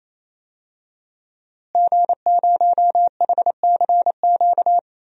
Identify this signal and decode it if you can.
Morse code, heard as a keyed tone, about 28 wpm, 700 Hz. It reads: G0HCQ
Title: DXCC-100 Call Signs - 5 Characters - Repeated 3x - 28wpm